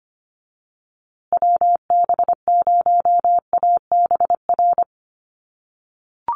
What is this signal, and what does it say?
Morse code, heard as a keyed tone, about 25 wpm, 700 Hz. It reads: WB0ABR
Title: USA Call Signs Repeated 3x 25wpm